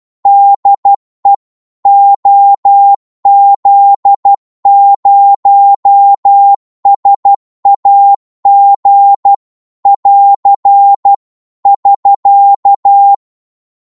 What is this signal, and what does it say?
Morse code, heard as a keyed tone, about 12 wpm, 800 Hz. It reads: DE OZ0SAG <AR> <SK>
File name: Morse / qso-12wpm